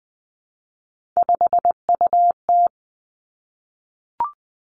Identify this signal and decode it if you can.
Morse code, heard as a keyed tone, about 20 wpm, 700 Hz. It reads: 5UT
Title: Three Challenging Letter-Number Combinations 20wpm